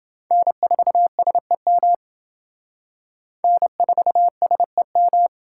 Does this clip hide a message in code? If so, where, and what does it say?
Yes: Morse code from 0.3 to 5.3 s, heard as a keyed tone, about 25 wpm, 700 Hz. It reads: N4SEM N4SEM